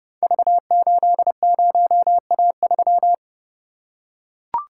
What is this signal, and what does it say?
Morse code, heard as a keyed tone, about 30 wpm, 700 Hz. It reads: V80A3
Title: Code Groups - 5 Letter:Numbers - 30wpm